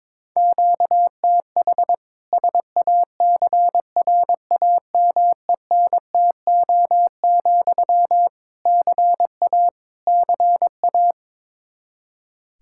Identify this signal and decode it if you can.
Morse code, heard as a keyed tone, about 22 wpm, 700 Hz. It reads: QTH SACRAMENTO, CA CA
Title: Practice QSO - Exchange 1 - Beginning - 22wpm